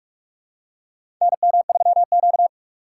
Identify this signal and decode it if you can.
Morse code, heard as a keyed tone, about 45 wpm, 700 Hz. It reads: NM3Q
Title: USA Call Signs - 4 Characters - 4x Character Spacing - 45wpm